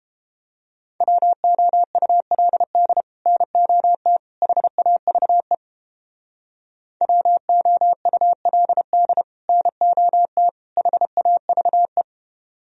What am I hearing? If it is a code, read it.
Morse code, heard as a keyed tone, about 31 wpm, 700 Hz. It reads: WOULD NOT HAVE WOULD NOT HAVE